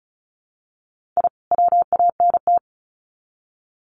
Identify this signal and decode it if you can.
Morse code, heard as a keyed tone, about 34 wpm, 700 Hz. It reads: I WANT
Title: Sets of 2 Words 35wpm